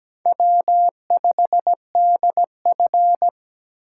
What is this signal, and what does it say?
Morse code, heard as a keyed tone, about 17 wpm, 700 Hz. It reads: W5DF
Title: USA Call Signs - 4 Characters - 3x Character Spacing - 17wpm